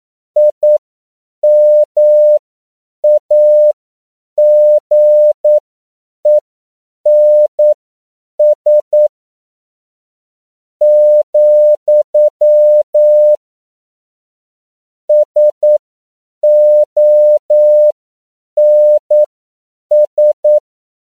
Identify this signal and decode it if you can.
Morse code, heard as a keyed tone, about 9 wpm, spaced as for 7 wpm, 600 Hz. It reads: IMAGENS , SONS
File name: CW / 10PPM